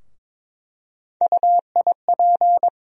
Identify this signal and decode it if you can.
Morse code, heard as a keyed tone, about 21 wpm, 700 Hz. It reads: UIP
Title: USA Call Sign Postfixes - 3 Letters 22wpm